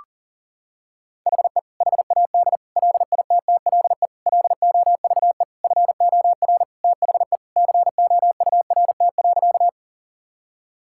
Morse code, heard as a keyed tone, about 40 wpm, 700 Hz. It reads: HE HAD LITTLE LOVE FOR THE COURT.
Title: Sentences from Top 1600 Words 40wpm